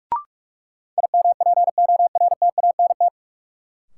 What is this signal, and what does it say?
Morse code, heard as a keyed tone, about 46 wpm, 700 Hz. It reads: IMPORTANT
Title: Sets of 3 Words - New Word Drill - 45wpm